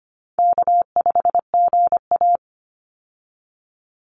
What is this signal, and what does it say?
Morse code, heard as a keyed tone, about 25 wpm, 700 Hz. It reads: K5GA
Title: Easy Call Signs Repeated 2x - 25wpm